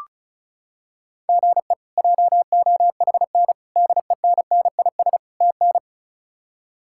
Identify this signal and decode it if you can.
Morse code, heard as a keyed tone, about 34 wpm, 700 Hz. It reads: GE JOHN DENNIS TN
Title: K1USN SST QSO Practice - 35wpm